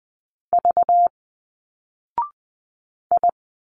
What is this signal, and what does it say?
Morse code, heard as a keyed tone, about 20 wpm, 700 Hz. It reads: V I